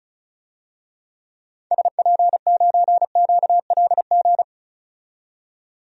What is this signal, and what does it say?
Morse code, heard as a keyed tone, about 36 wpm, 700 Hz. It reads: SP9QLG
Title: DXCC-100 Call Signs - 6 Characters - 35wpm